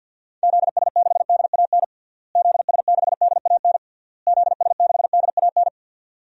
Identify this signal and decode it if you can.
Morse code, heard as a keyed tone, about 50 wpm, 700 Hz. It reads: ZS6DAN ZS6DAN ZS6DAN
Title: DXCC-100 Call Signs - 6 Characters - Repeated 3x - 50wpm